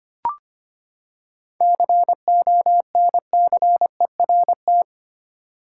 Morse code, heard as a keyed tone, about 25 wpm, 700 Hz. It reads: CONCERT
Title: Seven Letter Words - 25wpm